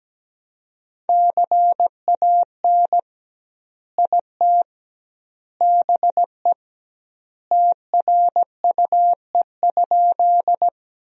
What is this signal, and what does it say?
Morse code, heard as a keyed tone, about 17 wpm, 700 Hz. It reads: CAN IT BE TRUE?
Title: 2x Word-Spacing - Sentences from Top 400 Words - 17wpm